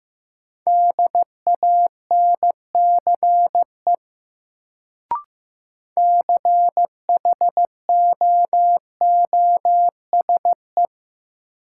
Morse code, heard as a keyed tone, about 15 wpm, 700 Hz. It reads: DANCE CHOOSE
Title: Sentences from Top 700 Words - New Word Drill - 15wpm